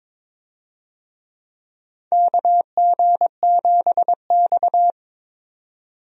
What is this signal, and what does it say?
Morse code, heard as a keyed tone, about 22 wpm, 700 Hz. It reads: KG7X
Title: USA Call Signs - 4 Characters - 22wpm